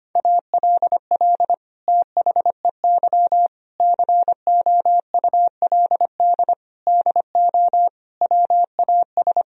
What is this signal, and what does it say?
Morse code, heard as a keyed tone, about 25 wpm, 700 Hz. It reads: ALL THEY COULD DO WAS
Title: Sentences from Top 1000 Words - Review Part 1 - 25wpm